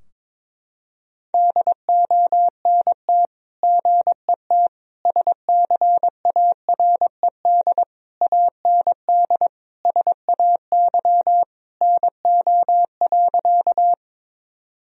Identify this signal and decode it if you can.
Morse code, heard as a keyed tone, about 22 wpm, 700 Hz. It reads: DONT GET SCARED AND SAY NO.
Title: Sentences from Top 1000 Words - Review Part 4 - 22wpm